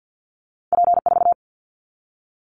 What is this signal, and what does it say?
Morse code, heard as a keyed tone, about 50 wpm, 700 Hz. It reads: P4
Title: Callsign Prefixes for Top 100 DXCC - 50wpm